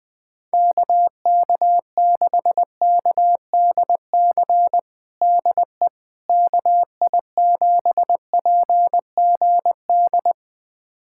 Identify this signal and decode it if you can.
Morse code, heard as a keyed tone, about 20 wpm, 700 Hz. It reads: KK6KDC DE KI7PGD
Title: USA Callsign DE Callsign 20wpm